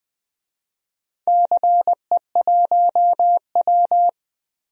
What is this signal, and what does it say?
Morse code, heard as a keyed tone, about 20 wpm, 700 Hz. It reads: CE1W
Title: DXCC-100 Call Signs - 4 Characters - Repeated 2x - 20wpm